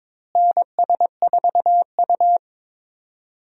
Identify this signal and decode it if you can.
Morse code, heard as a keyed tone, about 22 wpm, 700 Hz. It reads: NS4U
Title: USA Call Signs - 4 Characters - Repeated 3x - 22wpm